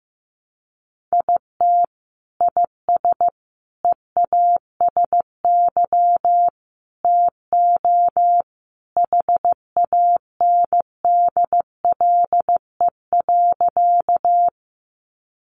Morse code, heard as a keyed tone, about 15 wpm, 700 Hz. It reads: IT IS EASY TO HANDLE.
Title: Sentences from Top 1100 Words 15wpm